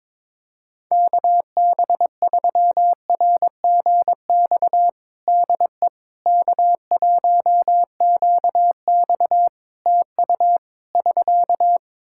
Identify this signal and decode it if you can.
Morse code, heard as a keyed tone, about 21 wpm, 700 Hz. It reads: KB3RGX DE K1QX TU <SK>
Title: Practice QSO - Exchange 4 - Ending - 22wpm